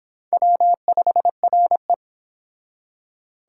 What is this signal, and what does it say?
Morse code, heard as a keyed tone, about 25 wpm, 700 Hz. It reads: W5RE